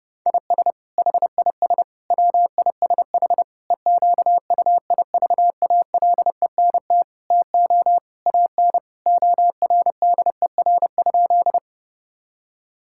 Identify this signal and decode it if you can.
Morse code, heard as a keyed tone, about 30 wpm, 700 Hz. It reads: IS HIS WISH EQUIVALENT TO AN ORDER?
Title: Sentences from Top 1600 Words 30wpm